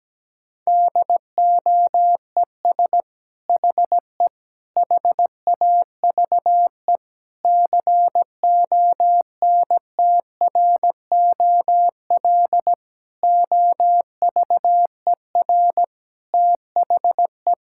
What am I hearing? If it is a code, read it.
Morse code, heard as a keyed tone, about 17 wpm, 700 Hz. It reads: DOES HE HAVE CONTROL OVER THE
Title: Sentences from Top 1000 Words - Review Part 5 - 17wpm